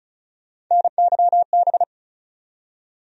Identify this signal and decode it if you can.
Morse code, heard as a keyed tone, about 36 wpm, 700 Hz. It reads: NYB
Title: USA Call Sign Postfixes - 3 Letters 35wpm